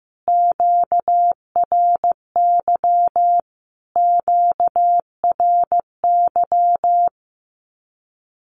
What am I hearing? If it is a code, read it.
Morse code, heard as a keyed tone, about 15 wpm, 700 Hz. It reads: QRY QRY
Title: Q-Codes Repeated 3x 15wpm